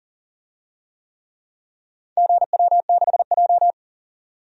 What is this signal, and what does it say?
Morse code, heard as a keyed tone, about 40 wpm, 700 Hz. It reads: GW6J
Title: DXCC-100 Call Signs - 4 Characters - 40wpm